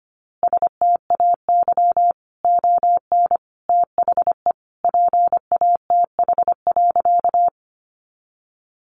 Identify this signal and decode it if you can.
Morse code, heard as a keyed tone, about 25 wpm, 700 Hz. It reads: STAY ON THE PATH.
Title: A Sound Of Thunder 25wpm